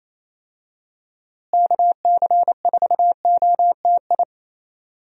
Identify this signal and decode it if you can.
Morse code, heard as a keyed tone, about 27 wpm, 700 Hz. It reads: KC4OTI
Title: USA Call Signs - 6 Characters - Repeated 2x - 28wpm